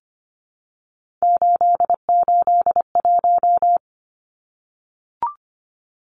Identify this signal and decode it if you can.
Morse code, heard as a keyed tone, about 25 wpm, 700 Hz. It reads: 881
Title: Numbers - 3-Digit 25wpm